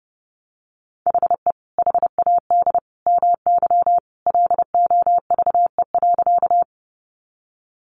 Morse code, heard as a keyed tone, about 30 wpm, 700 Hz. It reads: HE HAD MY LOVE.
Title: Sentences from Top 400 Words 30wpm